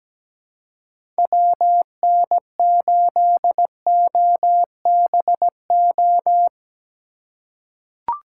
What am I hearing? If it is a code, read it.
Morse code, heard as a keyed tone, about 17 wpm, 700 Hz. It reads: WN8OBO
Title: USA Call Signs - 6 Characters - 4x Character Spacing - 17wpm